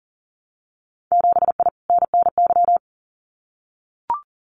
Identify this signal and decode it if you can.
Morse code, heard as a keyed tone, about 40 wpm, 700 Hz. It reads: 7I NNY
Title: Winter Field Day Exchanges Encore 40wpm